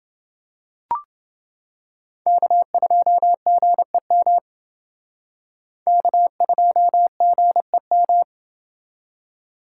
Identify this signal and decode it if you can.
Morse code, heard as a keyed tone, about 28 wpm, 700 Hz. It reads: K2GEM K2GEM